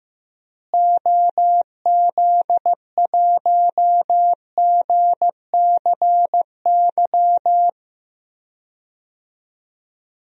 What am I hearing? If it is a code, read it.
Morse code, heard as a keyed tone, about 15 wpm, 700 Hz. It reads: OZ1GCY
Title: DXCC-100 Call Signs - 6 Characters - Repeated 3x - 15wpm